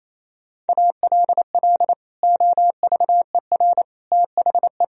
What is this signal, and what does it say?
Morse code, heard as a keyed tone, about 37 wpm, 700 Hz. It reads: ALL OVER THE